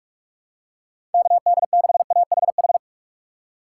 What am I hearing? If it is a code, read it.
Morse code, heard as a keyed tone, about 46 wpm, 700 Hz. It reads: KD6AHH
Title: USA Call Signs - 6 Characters - 2x Character Spacing - 45wpm